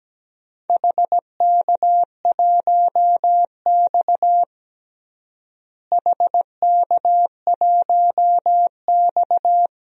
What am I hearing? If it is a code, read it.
Morse code, heard as a keyed tone, about 17 wpm, 700 Hz. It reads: HK1X HK1X
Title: DXCC-100 Call Signs - 4 Characters - Repeated 2x - 17wpm